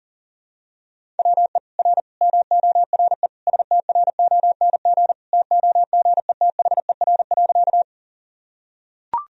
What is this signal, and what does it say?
Morse code, heard as a keyed tone, about 40 wpm, 700 Hz. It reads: WE R MORE STRONG TOGETHER.
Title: Sentences from Top 500 Words and Abbreviations 40wpm